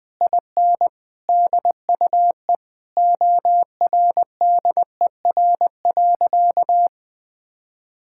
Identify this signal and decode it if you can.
Morse code, heard as a keyed tone, about 20 wpm, 700 Hz. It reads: IN DUE ORDER.
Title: Sentences from Top 1000 Words - Review Part 6 - 20wpm